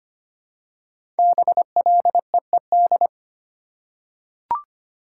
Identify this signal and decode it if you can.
Morse code, heard as a keyed tone, about 25 wpm, 700 Hz. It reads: BLEED
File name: 5 Letter Words - First 3 Letters Given - 25wpm